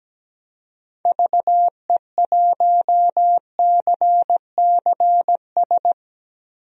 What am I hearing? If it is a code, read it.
Morse code, heard as a keyed tone, about 17 wpm, 700 Hz. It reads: VE1CCS
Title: DXCC-100 Call Signs - 6 Characters - Repeated 2x - 17wpm